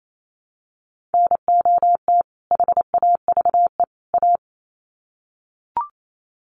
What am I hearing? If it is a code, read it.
Morse code, heard as a keyed tone, about 28 wpm, 700 Hz. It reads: NOT HAVE A
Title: Sets of 3 Words 28wpm